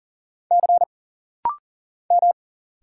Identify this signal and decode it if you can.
Morse code, heard as a keyed tone, about 40 wpm, 700 Hz. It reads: C M
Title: Single Letter-Number Alt - Rapid-Fire 40wpm